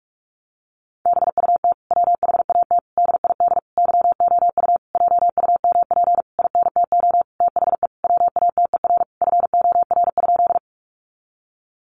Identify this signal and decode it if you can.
Morse code, heard as a keyed tone, about 44 wpm, 700 Hz. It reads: BUT WHAT DID YOU JUMP INTO THE WATER FOR?
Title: Sentences from Top 1300 Words 45wpm